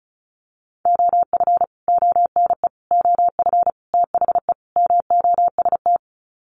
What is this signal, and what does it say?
Morse code, heard as a keyed tone, about 34 wpm, 700 Hz. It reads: OF ONE OF THE MOST
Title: Sets of 5 Words Encore2 35wpm